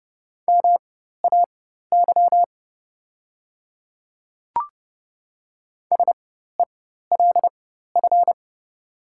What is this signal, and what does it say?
Morse code, heard as a keyed tone, about 30 wpm, spaced as for 14 wpm, 690 Hz. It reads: MAY SELF
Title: 4x Character Spacing - Top 200 Words - 30wpm